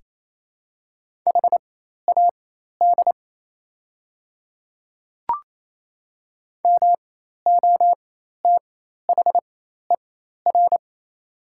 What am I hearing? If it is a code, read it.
Morse code, heard as a keyed tone, about 28 wpm, spaced as for 13 wpm, 690 Hz. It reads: HAD MOTHER
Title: 4x Character Spacing - Top 300 Words - 28wpm